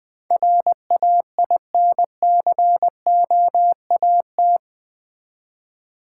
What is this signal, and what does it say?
Morse code, heard as a keyed tone, about 20 wpm, 700 Hz. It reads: RAINCOAT
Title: Easy Words with Hints - 20wpm